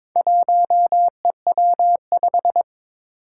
Morse code, heard as a keyed tone, about 22 wpm, 700 Hz. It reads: 1EW5
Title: Four Challenging Letter-Number Combinations 22wpm